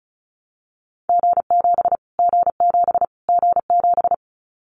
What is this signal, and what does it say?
Morse code, heard as a keyed tone, about 34 wpm, 700 Hz. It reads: G7 G7 G7
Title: Callsign Prefixes for Top 25 DXCC - Repeated 3x - 35wpm